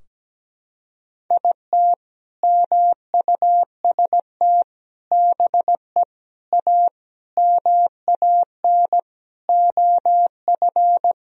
Morse code, heard as a keyed tone, about 17 wpm, 700 Hz. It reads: IT MUST BE A MAN OF
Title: Sentences from Top 1000 Words - Review Part 3 - 17wpm